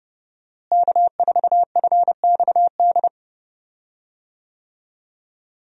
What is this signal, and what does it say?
Morse code, heard as a keyed tone, about 30 wpm, 700 Hz. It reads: K4FXD
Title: USA Call Signs - 5 Characters - Repeated 3x - 30wpm